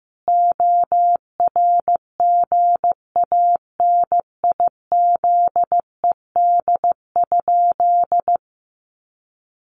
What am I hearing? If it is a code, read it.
Morse code, heard as a keyed tone, about 14 wpm, 700 Hz. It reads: ORGANIZED?
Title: Sentences from Top 1300 Words 15wpm